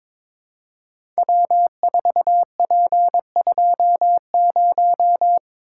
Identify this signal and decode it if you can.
Morse code, heard as a keyed tone, about 22 wpm, 700 Hz. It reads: W4P20
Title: Code Groups - 5 Letter:Numbers - 22wpm